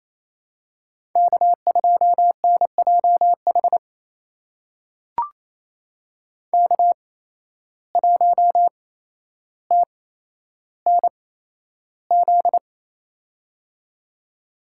Morse code, heard as a keyed tone, about 28 wpm, spaced as for 12 wpm, 700 Hz. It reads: K2NJH K1TNZ
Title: USA Call Signs - 5 Characters - 8x Character Spacing - 28wpm